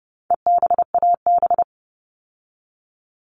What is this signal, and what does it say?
Morse code, heard as a keyed tone, about 30 wpm, 700 Hz. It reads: EBAB
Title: All Cut Numbers - 4-Digit 30wpm